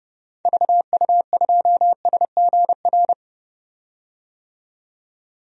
Morse code, heard as a keyed tone, about 30 wpm, 700 Hz. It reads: VU2SGR
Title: DXCC-100 Call Signs - 6 Characters - 30wpm